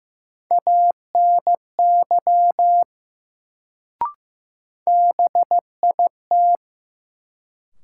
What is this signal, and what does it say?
Morse code, heard as a keyed tone, about 15 wpm, 700 Hz. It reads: ANY BIT
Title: Sets of 3 Words - New Word Drill - 15wpm